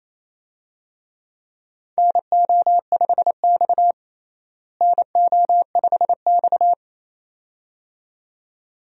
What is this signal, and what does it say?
Morse code, heard as a keyed tone, about 28 wpm, 700 Hz. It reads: NO5X NO5X
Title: USA Call Signs - 4 Characters - Repeated 2x - 28wpm